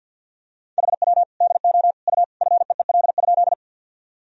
Morse code, heard as a keyed tone, about 46 wpm, 700 Hz. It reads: HW DO U FEEL?
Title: Sentences from Top 400 Words and Abbreviations 50wpm